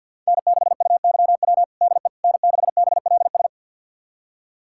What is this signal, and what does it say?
Morse code, heard as a keyed tone, about 49 wpm, 700 Hz. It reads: N6UYW DE N6BLS
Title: USA Callsign DE Callsign - 5 Characters - 50wpm